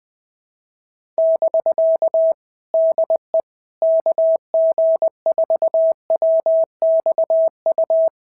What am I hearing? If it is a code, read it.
Morse code, heard as a keyed tone, about 20 wpm, 650 Hz. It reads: <BK> DE KG4WXU
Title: POTA QSO Practice - 20wpm